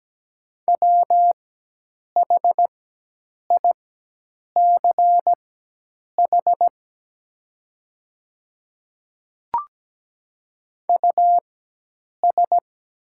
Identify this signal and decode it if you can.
Morse code, heard as a keyed tone, about 17 wpm, spaced as for 8 wpm, 700 Hz. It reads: WHICH US
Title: 4x Character Spacing - Top 200 Words - 17wpm